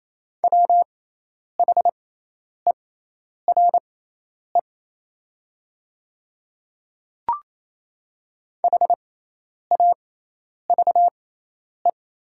Morse code, heard as a keyed tone, about 28 wpm, spaced as for 10 wpm, 690 Hz. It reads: WHERE HAVE
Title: 6x Character Spacing - Top 200 Words - 28wpm